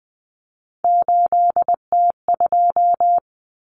Morse code, heard as a keyed tone, about 20 wpm, 700 Hz. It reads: 8T2
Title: Common Cut Numbers - 3-Digit 20wpm